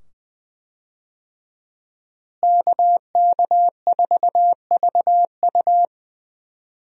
Morse code, heard as a keyed tone, about 20 wpm, 700 Hz. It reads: KK4VU
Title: USA Call Signs - 5 Characters - Repeated 2x - 20wpm